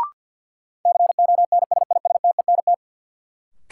Morse code, heard as a keyed tone, about 53 wpm, 700 Hz. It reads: CONSISTENT
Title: Sentences from Top 1500 Words - New Word Drill - 50wpm